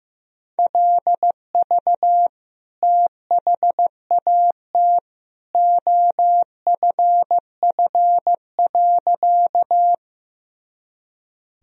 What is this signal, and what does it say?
Morse code, heard as a keyed tone, about 16 wpm, 700 Hz. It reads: LV THAT OFF.
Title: Sentences from Top 400 Words and Abbreviations 15wpm